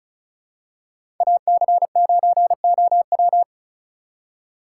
Morse code, heard as a keyed tone, about 36 wpm, 700 Hz. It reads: AC9OW
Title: USA Call Signs - 6x Character Spacing - 35wpm